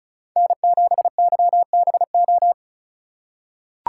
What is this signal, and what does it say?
Morse code, heard as a keyed tone, about 36 wpm, 700 Hz. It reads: N7YBO
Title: USA Call Signs - 5 Characters - 2x Character Spacing - 35wpm